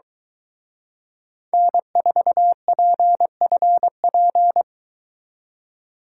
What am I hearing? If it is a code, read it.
Morse code, heard as a keyed tone, about 22 wpm, 700 Hz. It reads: N4PFP